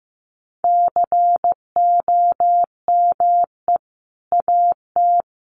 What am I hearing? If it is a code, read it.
Morse code, heard as a keyed tone, about 15 wpm, 700 Hz. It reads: COME AT
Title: Sentences from Top 300 Words 15wpm